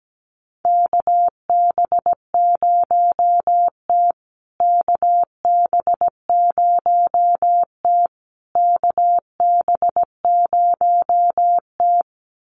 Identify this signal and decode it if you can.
Morse code, heard as a keyed tone, about 17 wpm, 700 Hz. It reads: KB0T KB0T KB0T
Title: USA Call Signs Repeated 3x 17wpm